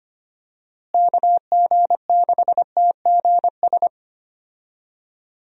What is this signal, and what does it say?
Morse code, heard as a keyed tone, about 25 wpm, 700 Hz. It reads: KG6TGS
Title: USA Call Signs - 6 Characters - Repeated 2x - 25wpm